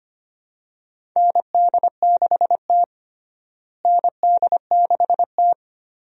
Morse code, heard as a keyed tone, about 25 wpm, 700 Hz. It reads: ND6T ND6T
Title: USA Call Signs - 4 Characters - Repeated 2x - 25wpm